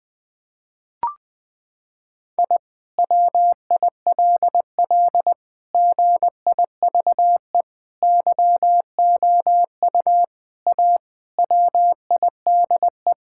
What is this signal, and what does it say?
Morse code, heard as a keyed tone, about 20 wpm, 700 Hz. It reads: I WILL GIVE YOU A WIDE
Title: Sentences from Top 1000 Words - Review Part 6 - 20wpm